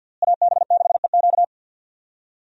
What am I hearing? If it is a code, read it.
Morse code, heard as a keyed tone, about 51 wpm, 700 Hz. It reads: AB6EQ
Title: USA Call Signs - 5 Characters - Repeated 3x - 50wpm